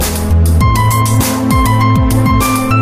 There is no code.